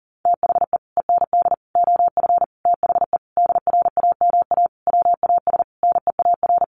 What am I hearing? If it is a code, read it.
Morse code, heard as a keyed tone, about 40 wpm, 700 Hz. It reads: THE END OF THE DRAMA WAS NEAR
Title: Sentences from Top 1500 Words 40wpm